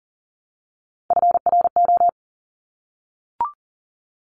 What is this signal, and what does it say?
Morse code, heard as a keyed tone, about 42 wpm, 690 Hz. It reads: FRO
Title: Top Three Letter Combinations in Words 40wpm